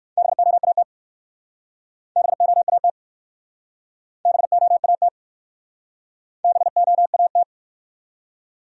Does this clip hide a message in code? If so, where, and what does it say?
Yes: Morse code from 0.2 to 7.4 s, heard as a keyed tone, about 59 wpm, 700 Hz. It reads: BOAT BOAT BOAT BOAT